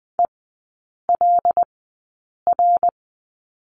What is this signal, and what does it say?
Morse code, heard as a keyed tone, about 20 wpm, spaced as for 8 wpm, 690 Hz. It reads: ELR
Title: 159 Lesson 26e - Letter g - Three Characters 20wpm